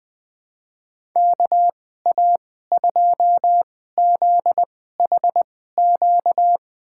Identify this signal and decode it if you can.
Morse code, heard as a keyed tone, about 21 wpm, spaced as for 15 wpm, 700 Hz. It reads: KA2ZHQ
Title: USA Call Signs - 2x Character Spacing - 20wpm